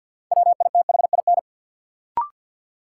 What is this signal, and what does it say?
Morse code, heard as a keyed tone, about 53 wpm, 700 Hz. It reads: WITHIN
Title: Sets of 2 Words - New Word Drill - 50wpm